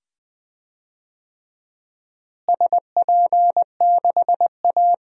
Morse code, heard as a keyed tone, about 21 wpm, 700 Hz. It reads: SP6A
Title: DXCC-100 Call Signs - 4 Characters - Repeated 2x - 20wpm